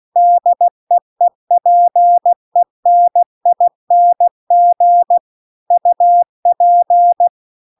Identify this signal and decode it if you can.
Morse code, heard as a keyed tone, about 16 wpm, 700 Hz. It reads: DEEPENING UP